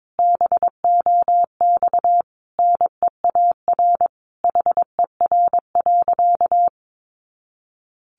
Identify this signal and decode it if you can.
Morse code, heard as a keyed tone, about 22 wpm, 700 Hz. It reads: BOX NEAR HER.
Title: Easy Phrases 22wpm